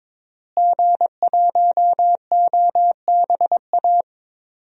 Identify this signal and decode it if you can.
Morse code, heard as a keyed tone, about 22 wpm, 700 Hz. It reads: G1OBA
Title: DXCC-100 Call Signs - 5 Characters - Repeated 3x - 22wpm